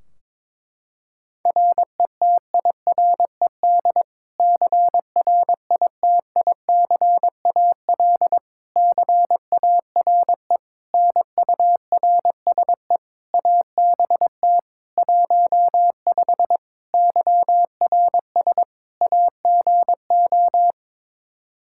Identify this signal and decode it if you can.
Morse code, heard as a keyed tone, about 22 wpm, 700 Hz. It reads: RETIRED CRITICAL CARE NURSE ABT 15 YRS AGO
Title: Practice QSO - Exchange 3 - More Info - 22wpm